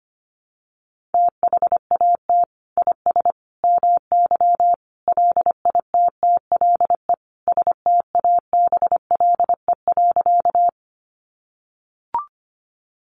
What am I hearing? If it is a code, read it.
Morse code, heard as a keyed tone, about 25 wpm, 700 Hz. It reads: THAT IS MY LITTLE STABLE.
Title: Sentences from Top 1500 Words 25wpm